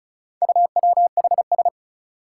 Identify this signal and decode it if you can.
Morse code, heard as a keyed tone, about 36 wpm, 700 Hz. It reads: UWHS
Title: Four Challenging Letter-Number Combinations 35wpm